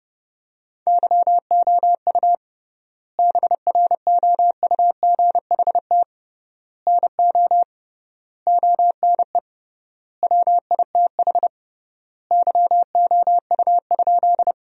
Morse code, heard as a keyed tone, about 30 wpm, 700 Hz. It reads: YOU BROUGHT NO ONE WITH YOU?
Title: 3x Word-Spacing - Sentences from Top 500 Words - 30wpm